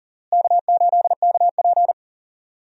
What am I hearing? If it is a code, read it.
Morse code, heard as a keyed tone, about 40 wpm, 700 Hz. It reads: K8KP
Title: USA Call Signs - 4 Characters - Repeated 3x - 40wpm